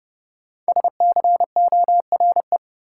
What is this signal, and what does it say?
Morse code, heard as a keyed tone, about 30 wpm, 700 Hz. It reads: SCORE
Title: Sentences from Top 1600 Words - New Word Drill - 30wpm